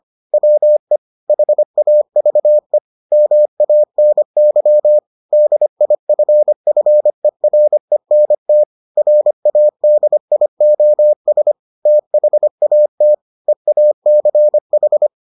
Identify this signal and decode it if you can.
Morse code, heard as a keyed tone, about 25 wpm, 600 Hz. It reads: WE HAVE MANY DIFFERENT RADIOS THAT EACH